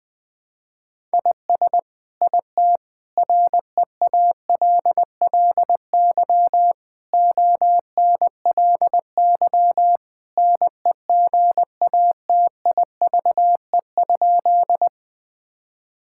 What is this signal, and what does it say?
Morse code, heard as a keyed tone, about 21 wpm, 700 Hz. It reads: IS IT REALLY ONLY NEGATIVE?
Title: Sentences from Top 1000 Words - Review Part 6 - 20wpm